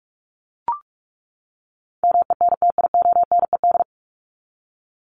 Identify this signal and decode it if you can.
Morse code, heard as a keyed tone, about 44 wpm, 700 Hz. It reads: MENTIONED
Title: ED Suffix - Top 100 Words with Base Words 45wpm